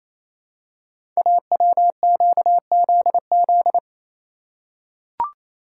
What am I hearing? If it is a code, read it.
Morse code, heard as a keyed tone, about 29 wpm, 700 Hz. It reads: AWQZZ
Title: Code Groups - 5 Letters - 28wpm